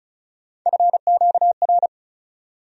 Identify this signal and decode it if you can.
Morse code, heard as a keyed tone, about 34 wpm, 700 Hz. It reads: FQR
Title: USA Call Sign Postfixes - 3 Letters 35wpm